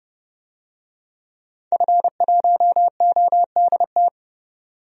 Morse code, heard as a keyed tone, about 30 wpm, 700 Hz. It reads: F1ODT
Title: DXCC-100 Call Signs - 5 Characters - 30wpm